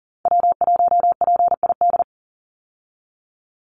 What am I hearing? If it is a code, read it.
Morse code, heard as a keyed tone, about 40 wpm, 700 Hz. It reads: W1PID
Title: Call Signs Repeated 3x - 40wpm